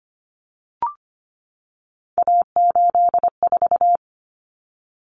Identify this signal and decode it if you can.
Morse code, heard as a keyed tone, about 25 wpm, 700 Hz. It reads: A84
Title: Common Cut Numbers - 3-Digit 25wpm